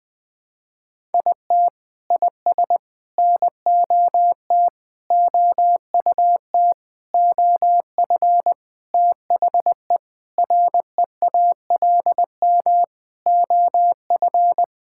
Morse code, heard as a keyed tone, about 20 wpm, 700 Hz. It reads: IT IS NOT OUT OF THE REALM OF
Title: Sentences with 9-Word Phrases - Spoken Beginning - 20wpm